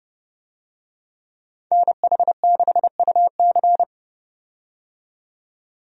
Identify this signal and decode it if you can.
Morse code, heard as a keyed tone, about 31 wpm, 700 Hz. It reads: NH6UC
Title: USA Call Signs - 5 Characters - 30wpm